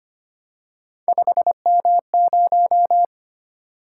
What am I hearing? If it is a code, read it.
Morse code, heard as a keyed tone, about 25 wpm, 700 Hz. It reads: 5M0
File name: Three Letter-Number Combinations - 25wpm